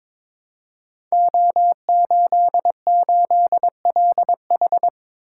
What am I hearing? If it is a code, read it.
Morse code, heard as a keyed tone, about 22 wpm, 700 Hz. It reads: O88LH
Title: Code Groups - 5 Letter:Numbers - 22wpm